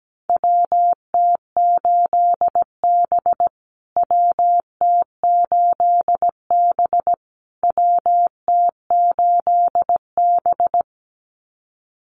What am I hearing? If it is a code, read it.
Morse code, heard as a keyed tone, about 17 wpm, 700 Hz. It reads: WT8B WT8B WT8B
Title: USA Call Signs Repeated 3x 17wpm